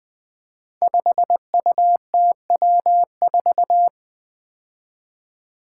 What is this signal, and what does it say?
Morse code, heard as a keyed tone, about 20 wpm, 700 Hz. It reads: 5UTW4
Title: Five Challenging Letter-Number Combinations 20wpm